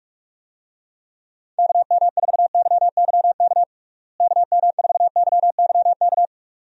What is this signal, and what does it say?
Morse code, heard as a keyed tone, about 46 wpm, 700 Hz. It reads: KM4YYK KM4YYK
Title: USA Call Signs - 6 Characters - Repeated 2x - 45wpm